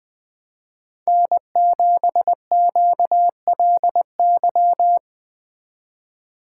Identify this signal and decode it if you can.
Morse code, heard as a keyed tone, about 20 wpm, 700 Hz. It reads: N7QLY
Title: Speed Racing - USA 1x3 Call Signs 17wpm